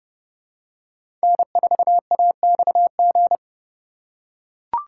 Morse code, heard as a keyed tone, about 30 wpm, 700 Hz. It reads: N4AXG
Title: USA Call Signs - 5 Characters - 4x Character Spacing - 30wpm